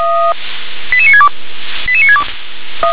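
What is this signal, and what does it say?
Five-tone selective call (ZVEI), heard as quick runs of five tones, tones of about 80 ms. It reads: 88962 88962
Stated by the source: ZVEI five-tone selective call received by SDR.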